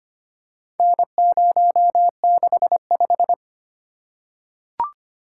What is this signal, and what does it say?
Morse code, heard as a keyed tone, about 25 wpm, 700 Hz. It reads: N065
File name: practice / 25wpm / Four Letter-Number Combinations - 25wpm